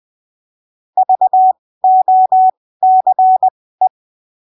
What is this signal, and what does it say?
Morse code, heard as a keyed tone, about 20 wpm, spaced as for 15 wpm, 750 Hz. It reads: VOCE
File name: arquivos / texto / texto 16 20ppm